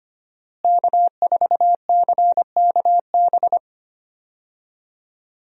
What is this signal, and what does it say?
Morse code, heard as a keyed tone, about 25 wpm, 700 Hz. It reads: K4CKB
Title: USA Call Signs - 5 Characters - Repeated 3x - 25wpm